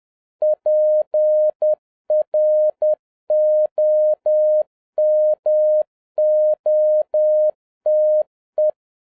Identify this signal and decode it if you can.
Morse code, heard as a keyed tone, about 10 wpm, 610 Hz. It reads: PROMOTE